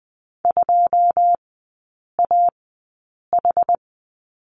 Morse code, heard as a keyed tone, about 20 wpm, spaced as for 8 wpm, 700 Hz. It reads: 2AH
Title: 081 Lesson 14e - Number 2 - Three Characters 20wpm